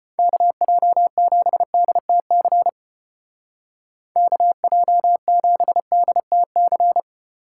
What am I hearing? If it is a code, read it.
Morse code, heard as a keyed tone, about 31 wpm, 700 Hz. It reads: KJ7DTC KJ7DTC